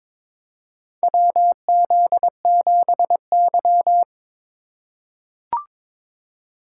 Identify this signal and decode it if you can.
Morse code, heard as a keyed tone, about 22 wpm, 700 Hz. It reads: WZ7Y
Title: USA Call Signs - 3x Character Spacing - 22wpm